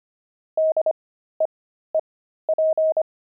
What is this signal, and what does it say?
Morse code, heard as a keyed tone, about 25 wpm, spaced as for 13 wpm, 630 Hz. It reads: DEEP
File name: pangram-fs-13-cs-25-deep